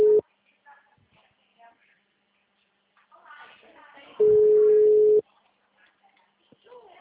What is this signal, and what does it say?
Ringback tone (Europe, 425 Hz), with speech.
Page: Cordeiropolis 3